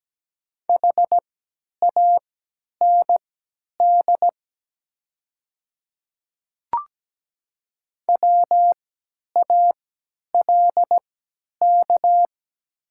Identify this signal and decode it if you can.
Morse code, heard as a keyed tone, about 17 wpm, spaced as for 10 wpm, 700 Hz. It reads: HAND WALK
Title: 3x Character Spacing - Top 300 Words - 17wpm